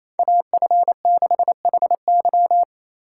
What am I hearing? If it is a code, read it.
Morse code, heard as a keyed tone, about 28 wpm, 700 Hz. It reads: AF6HY
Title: USA Call Signs - 4x Character Spacing - 28wpm